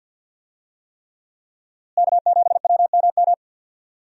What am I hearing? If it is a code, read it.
Morse code, heard as a keyed tone, about 47 wpm, 700 Hz. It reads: K7WMM
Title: USA Call Signs - 5 Characters - Repeated 3x - 50wpm